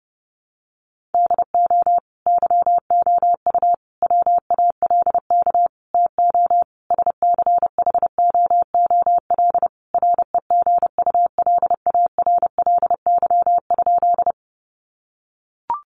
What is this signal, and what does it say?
Morse code, heard as a keyed tone, about 30 wpm, 700 Hz. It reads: DO YOU WALK TO SCHOOL REGULARLY?
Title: Sentences from Top 1400 Words 30wpm